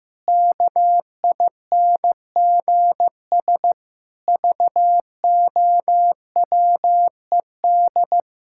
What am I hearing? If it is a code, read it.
Morse code, heard as a keyed tone, about 15 wpm, 700 Hz. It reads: KINGS VOWED
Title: Pangram Challenge v2 - 15wpm